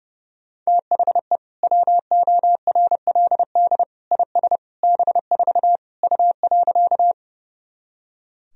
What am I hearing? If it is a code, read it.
Morse code, heard as a keyed tone, about 30 wpm, 700 Hz. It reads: THE WORLD IS B4 U.
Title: Sentences from Top 200 Words and Abbreviations 30wpm